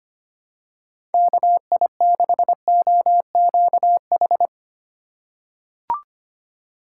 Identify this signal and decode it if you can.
Morse code, heard as a keyed tone, about 25 wpm, 700 Hz. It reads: KI6OQH
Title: USA Call Signs - 6 Characters - 3x Character Spacing - 25wpm